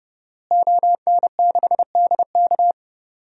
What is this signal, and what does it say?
Morse code, heard as a keyed tone, about 30 wpm, 700 Hz. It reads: ON6DK
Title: DXCC-100 Call Signs - 5 Characters - Repeated 3x - 30wpm